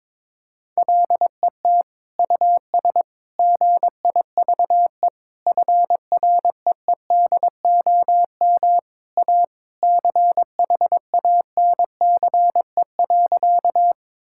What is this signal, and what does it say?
Morse code, heard as a keyed tone, about 22 wpm, 700 Hz. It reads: LET US GIVE FREEDOM A CHANCE.
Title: Sentences from Top 1000 Words - Review Part 5 - 22wpm